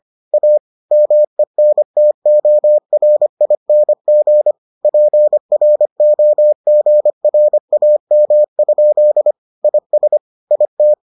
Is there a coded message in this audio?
Yes: Morse code from 0.3 to 10.9 s, heard as a keyed tone, about 25 wpm, 600 Hz. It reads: A MENTORING PROGRAM? IS IT